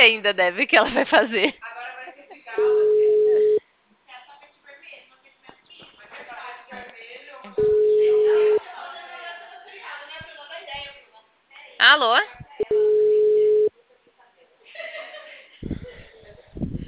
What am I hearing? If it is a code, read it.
Ringback tone (Europe, 425 Hz), with speech.